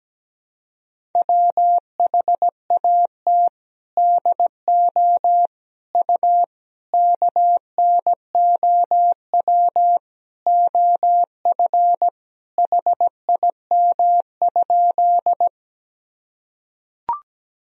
Morse code, heard as a keyed tone, about 17 wpm, 700 Hz. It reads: WHAT DO U KNOW OF HIM?
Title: Sentences from Top 100 Words and Abbreviations 17wpm